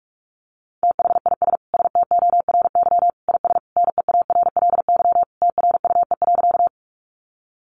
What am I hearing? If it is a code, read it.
Morse code, heard as a keyed tone, about 46 wpm, 700 Hz. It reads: THIS STORY IS NEARLY TRUE.
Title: Sentences from Top 1300 Words 45wpm